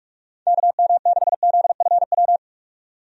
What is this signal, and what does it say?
Morse code, heard as a keyed tone, about 42 wpm, 700 Hz. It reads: KM6ZFW
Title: USA Call Signs - 6 Characters - 45wpm